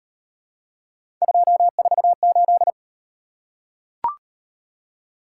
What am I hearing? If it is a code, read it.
Morse code, heard as a keyed tone, about 36 wpm, 700 Hz. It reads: 248